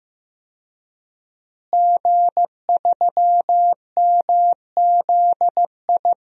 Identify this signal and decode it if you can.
Morse code, heard as a keyed tone, about 15 wpm, 700 Hz. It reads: G3MZI
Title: DXCC-100 Call Signs - 5 Characters - Repeated 3x - 15wpm